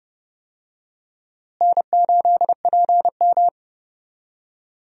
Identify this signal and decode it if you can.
Morse code, heard as a keyed tone, about 30 wpm, 700 Hz. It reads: N8PM
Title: USA Call Signs - 4 Characters - Repeated 2x - 30wpm